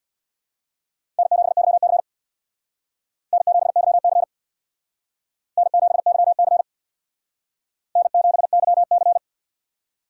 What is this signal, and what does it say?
Morse code, heard as a keyed tone, about 59 wpm, 700 Hz. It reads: N7YC N7YC N7YC N7YC
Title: Speed Racing - USA Call Signs 50wpm